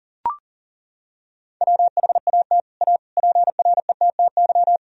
Morse code, heard as a keyed tone, about 37 wpm, 700 Hz. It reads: WHAT A PRETTY
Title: Sentences from Top 1000 Words - Review Part 2 - 40wpm